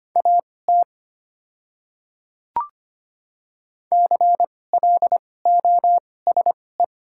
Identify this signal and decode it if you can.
Morse code, heard as a keyed tone, about 25 wpm, spaced as for 18 wpm, 700 Hz. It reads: AT CLOSE
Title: 2x Character Spacing - Top 300 Words - 25wpm